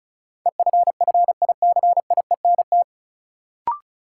Morse code, heard as a keyed tone, about 36 wpm, 700 Hz. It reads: EFFICIENT
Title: Sentences from Top 1300 Words - New Word Drill - 35wpm